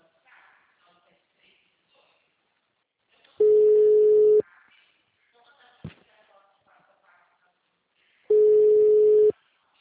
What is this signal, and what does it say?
Ringback tone (Europe, 425 Hz), with speech.